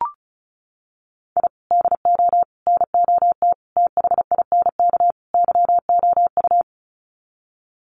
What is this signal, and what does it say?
Morse code, heard as a keyed tone, about 34 wpm, 700 Hz. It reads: I DO NOT THINK YOU
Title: Sets of 5 Words 35wpm